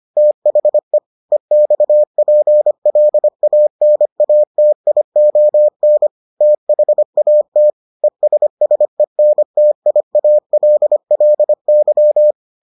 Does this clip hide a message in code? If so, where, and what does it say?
Yes: Morse code from 0.2 to 12.3 s, heard as a keyed tone, about 25 wpm, 600 Hz. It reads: THE EXPLANATION THAT ESSENTIALLY